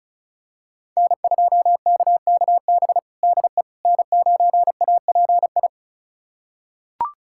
Morse code, heard as a keyed tone, about 34 wpm, 690 Hz. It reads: N2KKB DE N9API
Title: USA Callsign DE Callsign - 5 Characters - 35wpm